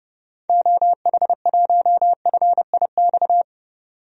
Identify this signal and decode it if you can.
Morse code, heard as a keyed tone, about 30 wpm, 700 Hz. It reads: OH1FIX
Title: DXCC-100 Call Signs - 6 Characters - 30wpm